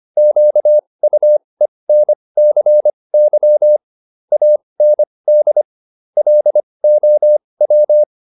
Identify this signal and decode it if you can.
Morse code, heard as a keyed tone, about 25 wpm, spaced as for 20 wpm, 600 Hz. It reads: QUENCY AND LOW